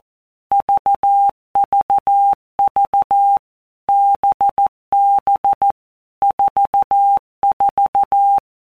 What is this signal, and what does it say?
Morse code, heard as a keyed tone, about 14 wpm, 800 Hz. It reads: VVV BB 44
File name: Basic Letter Groups By Blocks / Groups@13 / VB46@13